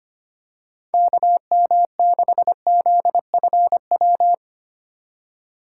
Morse code, heard as a keyed tone, about 25 wpm, 700 Hz. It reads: KM6ZFW
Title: USA Call Signs - 6 Characters - 3x Character Spacing - 25wpm